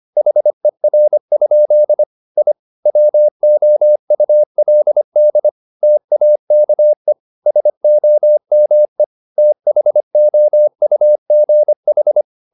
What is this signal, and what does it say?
Morse code, heard as a keyed tone, about 25 wpm, 600 Hz. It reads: HER? I WOULD TAKE SOME THOUGH